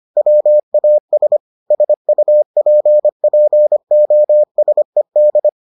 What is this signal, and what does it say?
Morse code, heard as a keyed tone, about 25 wpm, 600 Hz. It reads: WAS SUPPOSED